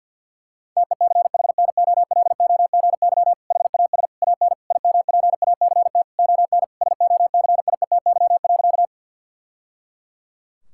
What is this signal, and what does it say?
Morse code, heard as a keyed tone, about 48 wpm, 700 Hz. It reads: TECHNOLOGY HAS AN IMPACT ON SOCIETY.
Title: Sentences from Top 1000 Words - Review Part 6 - 50wpm